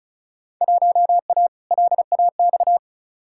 Morse code, heard as a keyed tone, about 34 wpm, 700 Hz. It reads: 1A LAX
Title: Field Day Exchanges Encore - 35wpm